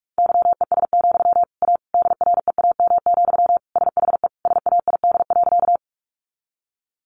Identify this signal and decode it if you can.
Morse code, heard as a keyed tone, about 44 wpm, 700 Hz. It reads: YES, A DREAM, SHE SAID.
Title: Sentences from Top 1500 Words 45wpm